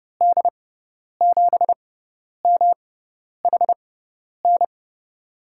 Morse code, heard as a keyed tone, about 30 wpm, spaced as for 10 wpm, 700 Hz. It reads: D7MHN
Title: USA Call Signs - 6 Characters - 6x Character Spacing - 30wpm